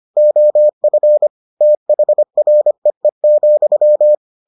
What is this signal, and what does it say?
Morse code, heard as a keyed tone, about 25 wpm, 600 Hz. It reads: OF THREE,